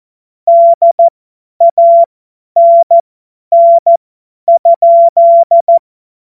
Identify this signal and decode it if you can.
Morse code, heard as a keyed tone, about 14 wpm, spaced as for 10 wpm, 680 Hz. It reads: DANN?
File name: Morsekurs / Witze / Witz 06 Februar 2024 50 F70